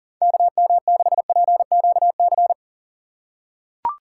Morse code, heard as a keyed tone, about 40 wpm, 700 Hz. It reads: KM6PQC
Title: USA Call Signs - 6 Characters - 4x Character Spacing - 40wpm